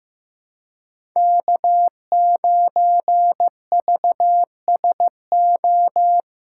Morse code, heard as a keyed tone, about 16 wpm, 700 Hz. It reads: K9VSO
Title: USA Callsign DE Callsign - 5 Characters - 15wpm